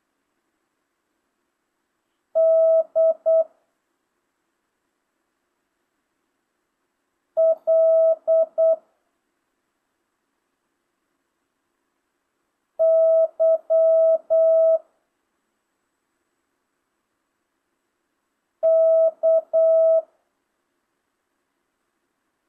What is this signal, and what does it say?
Morse code, heard as a keyed tone, about 8 wpm, 650 Hz. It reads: D L Y K
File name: jh8cbh / 2025cw